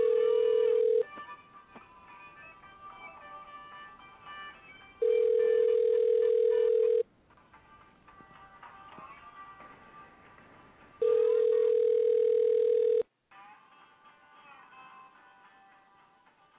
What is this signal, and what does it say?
Ringback tone (North America), 440 + 480 Hz.